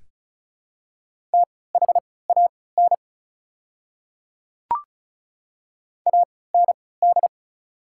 Morse code, heard as a keyed tone, about 36 wpm, spaced as for 20 wpm, 700 Hz. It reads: THAN AND
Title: 3x Character Spacing - Top 100 Words - 35wpm